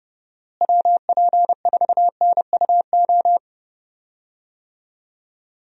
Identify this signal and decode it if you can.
Morse code, heard as a keyed tone, about 30 wpm, 700 Hz. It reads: WP4NUO
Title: USA Call Signs - 6 Characters - Repeated 3x - 30wpm